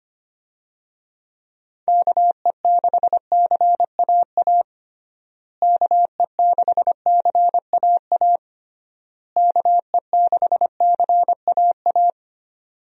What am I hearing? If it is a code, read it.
Morse code, heard as a keyed tone, about 25 wpm, 700 Hz. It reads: KE6CAA KE6CAA KE6CAA
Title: USA Call Signs - 6 Characters - Repeated 3x - 25wpm